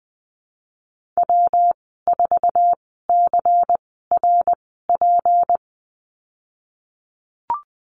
Morse code, heard as a keyed tone, about 20 wpm, spaced as for 15 wpm, 700 Hz. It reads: W4CRP
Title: 154 Lesson 25g - Letter p - Call Signs 20wpm